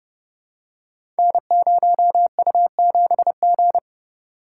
Morse code, heard as a keyed tone, about 30 wpm, 700 Hz. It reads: N0U7G
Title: Code Groups - 5 Letter:Numbers - 30wpm